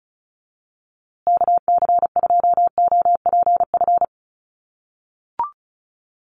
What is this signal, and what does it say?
Morse code, heard as a keyed tone, about 34 wpm, 700 Hz. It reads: KC2OPF
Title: USA Call Signs Repeated 2x 35wpm